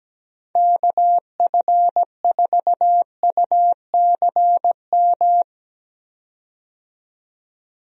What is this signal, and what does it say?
Morse code, heard as a keyed tone, about 17 wpm, 700 Hz. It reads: KF4UCM
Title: USA Call Signs - 6 Characters - 17wpm